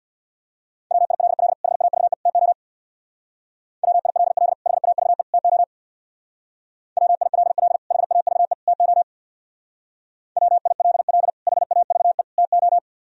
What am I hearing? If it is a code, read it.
Morse code, heard as a keyed tone, about 63 wpm, 700 Hz. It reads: WILL HAVE TO WILL HAVE TO WILL HAVE TO WILL HAVE TO